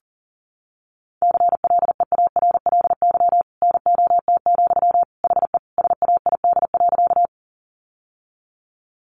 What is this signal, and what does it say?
Morse code, heard as a keyed tone, about 40 wpm, 700 Hz. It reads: CLEARLY NOT, HE SAID.
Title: Sentences from Top 1300 Words 40wpm